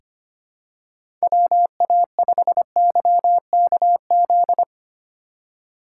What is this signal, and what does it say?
Morse code, heard as a keyed tone, about 25 wpm, 700 Hz. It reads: WA5YKZ
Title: USA Call Signs - 6x Character Spacing - 25wpm